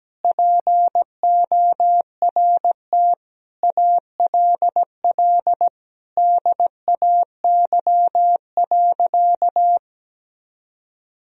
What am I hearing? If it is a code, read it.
Morse code, heard as a keyed tone, about 17 wpm, 700 Hz. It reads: PORT ALL DAY.
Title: Sentences from Top 1000 Words - Review Part 1 - 17wpm